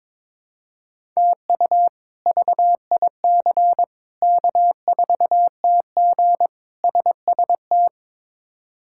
Morse code, heard as a keyed tone, about 22 wpm, 700 Hz. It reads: TU VIC K4TG SST
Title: K1USN SST QSO Practice - 22wpm